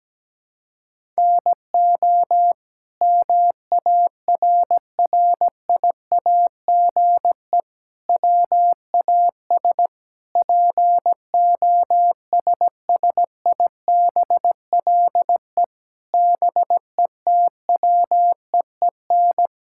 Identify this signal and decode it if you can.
Morse code, heard as a keyed tone, about 17 wpm, 700 Hz. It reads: NO MARRIAGE WAS POSSIBLE BETWEEN
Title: Sentences from Top 1000 Words - Review Part 5 - 17wpm